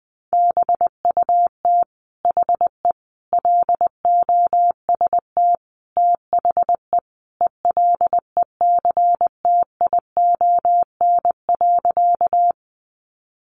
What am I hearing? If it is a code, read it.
Morse code, heard as a keyed tone, about 20 wpm, 700 Hz. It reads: BUT HE LOST THE ELECTION.
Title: Sentences from Top 1500 Words 20wpm